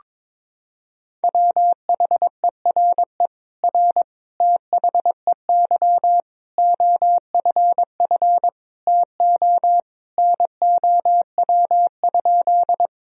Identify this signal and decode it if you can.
Morse code, heard as a keyed tone, about 22 wpm, 700 Hz. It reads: WHERE R THEY OFF TO NOW?
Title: Sentences from Top 200 Words and Abbreviations 22wpm